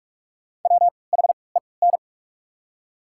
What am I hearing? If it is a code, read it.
Morse code, heard as a keyed tone, about 46 wpm, spaced as for 26 wpm, 700 Hz. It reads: WHEN
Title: 3x Character Spacing - Top 500 Words - 45wpm